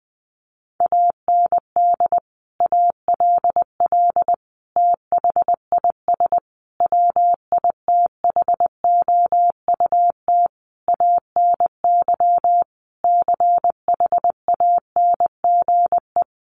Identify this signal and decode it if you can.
Morse code, heard as a keyed tone, about 20 wpm, 700 Hz. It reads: AND ALL THIS WITHOUT ANY CHANGE
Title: Sentences from Top 1200 Words 20wpm